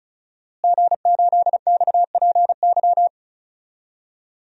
Morse code, heard as a keyed tone, about 36 wpm, 700 Hz. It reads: G8XPY
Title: DXCC-100 Call Signs - 5 Characters - Repeated 3x - 35wpm